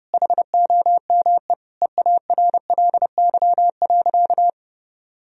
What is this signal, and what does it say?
Morse code, heard as a keyed tone, about 30 wpm, 700 Hz. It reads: HOME EARLY.
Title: Sentences from Top 1000 Words - Review Part 4 - 30wpm